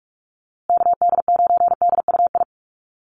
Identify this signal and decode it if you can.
Morse code, heard as a keyed tone, about 44 wpm, 690 Hz. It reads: KD9DUI
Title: USA Call Signs Repeated 3x 45wpm